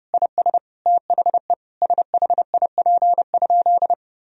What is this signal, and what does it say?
Morse code, heard as a keyed tone, about 30 wpm, 700 Hz. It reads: IS THE SHIP?
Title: Sentences from Top 1000 Words - Review Part 2 - 30wpm